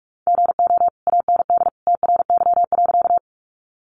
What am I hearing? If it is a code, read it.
Morse code, heard as a keyed tone, about 44 wpm, 700 Hz. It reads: GO AND TRY.
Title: Sentences from Top 200 Words 45wpm